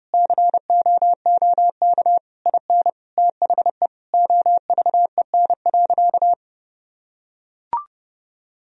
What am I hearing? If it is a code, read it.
Morse code, heard as a keyed tone, about 32 wpm, 700 Hz. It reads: COOK IN THE OVEN.
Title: Sentences from Top 1000 Words - Review Part 4 - 30wpm